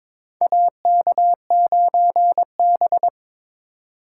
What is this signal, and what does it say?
Morse code, heard as a keyed tone, about 22 wpm, 700 Hz. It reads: AK9B
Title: USA Call Signs - 4 Characters - 2x Character Spacing - 22wpm